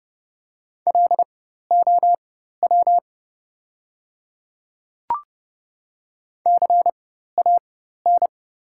Morse code, heard as a keyed tone, about 30 wpm, spaced as for 14 wpm, 700 Hz. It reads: LOW CAN
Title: 4x Character Spacing - Top 300 Words - 30wpm